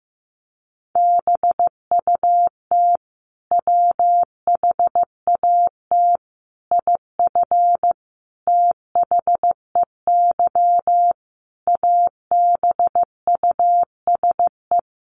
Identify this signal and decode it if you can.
Morse code, heard as a keyed tone, about 15 wpm, 700 Hz. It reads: BUT WHAT IF THEY ABUSE
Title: Sentences from Top 1400 Words 15wpm